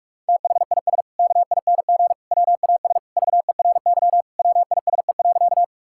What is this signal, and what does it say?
Morse code, heard as a keyed tone, about 43 wpm, 700 Hz. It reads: THIS KING WAS VERY WISE.
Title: Sentences from Top 1600 Words 45wpm